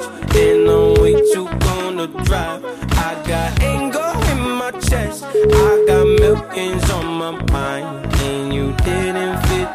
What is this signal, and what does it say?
Ringback tone (Europe, 425 Hz), with noise.